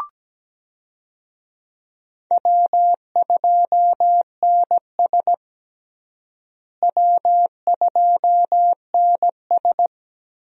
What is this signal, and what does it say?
Morse code, heard as a keyed tone, about 17 wpm, 700 Hz. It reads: W2NS W2NS
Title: USA Call Signs - 4 Characters - Repeated 3x - 17wpm